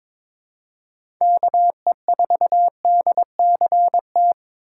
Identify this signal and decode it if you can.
Morse code, heard as a keyed tone, about 22 wpm, 700 Hz. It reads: KE4DCT
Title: USA Call Signs - 6 Characters - 2x Character Spacing - 22wpm